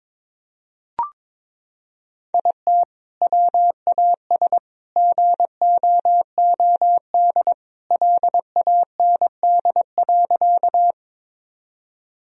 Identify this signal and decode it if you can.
Morse code, heard as a keyed tone, about 21 wpm, 700 Hz. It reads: IT WAS GOOD LAND.
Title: Sentences from Top 1000 Words - Review Part 1 - 22wpm